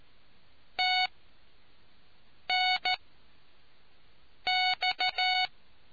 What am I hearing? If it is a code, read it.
Morse code, heard as a keyed tone, about 13 wpm, spaced as for 5 wpm, 740 Hz. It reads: TNX